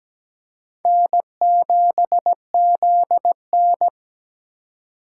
Morse code, heard as a keyed tone, about 17 wpm, 700 Hz. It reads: N7ZN
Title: K1USN SST QSO Practice - 17wpm